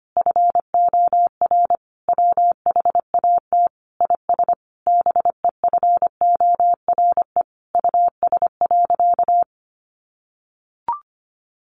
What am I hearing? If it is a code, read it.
Morse code, heard as a keyed tone, about 25 wpm, 700 Hz. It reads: FOR WHAT IS BEFORE US.
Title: Sentences from Top 600 Words 25wpm